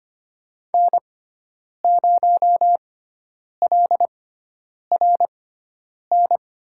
Morse code, heard as a keyed tone, about 25 wpm, spaced as for 9 wpm, 700 Hz. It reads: N0LRN
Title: USA Call Signs - 5 Characters - 6x Character Spacing - 25wpm